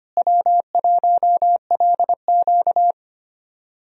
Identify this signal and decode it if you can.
Morse code, heard as a keyed tone, about 25 wpm, 700 Hz. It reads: W1LQ
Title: USA Call Signs - 4 Characters - Repeated 3x - 25wpm